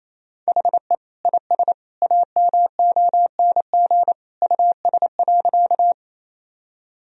Morse code, heard as a keyed tone, about 27 wpm, 700 Hz. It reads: HE IS AMONG US.
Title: Sentences from Top 1000 Words - Review Part 3 - 28wpm